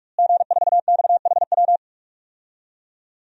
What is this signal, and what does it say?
Morse code, heard as a keyed tone, about 46 wpm, 700 Hz. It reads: G4XHW
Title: DXCC-100 Call Signs - 5 Characters - Repeated 2x - 45wpm